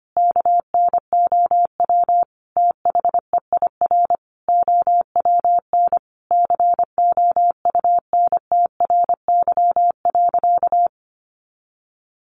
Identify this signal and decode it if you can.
Morse code, heard as a keyed tone, about 25 wpm, 700 Hz. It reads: KNOW THEIR OWN COUNTRY.
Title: Sentences from Top 300 Words 25wpm